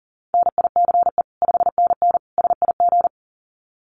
Morse code, heard as a keyed tone, about 40 wpm, 700 Hz. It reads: NICE 5NN SIG
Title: Common QSO Phrases 40wpm